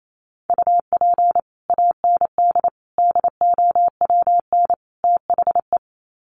Morse code, heard as a keyed tone, about 28 wpm, 700 Hz. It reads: UP AND DOWN THE
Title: Sets of 4 Words 28wpm